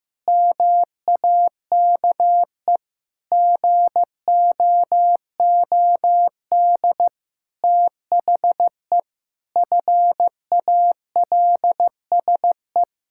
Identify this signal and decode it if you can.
Morse code, heard as a keyed tone, about 15 wpm, 700 Hz. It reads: MAKE GOOD THE FALSE
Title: Sentences from Top 1600 Words 15wpm